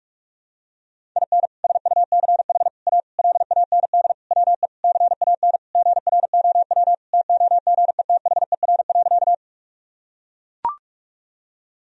Morse code, heard as a keyed tone, about 44 wpm, 700 Hz. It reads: IN SUCH A LAND WE CAN GROW TOGETHER.
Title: Sentences from Top 1000 Words - Review Part 1 - 45wpm